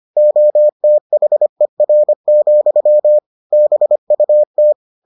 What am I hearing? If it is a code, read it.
Morse code, heard as a keyed tone, about 25 wpm, 600 Hz. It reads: OTHER, BUT